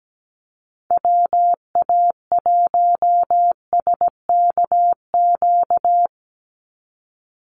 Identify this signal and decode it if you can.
Morse code, heard as a keyed tone, about 17 wpm, 700 Hz. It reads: WA1SKQ
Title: Call Signs - 17wpm